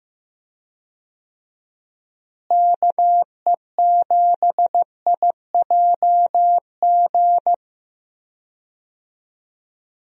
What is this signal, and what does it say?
Morse code, heard as a keyed tone, about 15 wpm, 700 Hz. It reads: KE7IJG
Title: USA Call Signs - 6 Characters - Repeated 2x - 15wpm